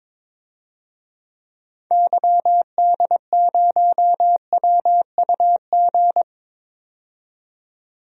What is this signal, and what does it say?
Morse code, heard as a keyed tone, about 21 wpm, 700 Hz. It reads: YD0WUG
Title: DXCC-100 Call Signs - 6 Characters - 22wpm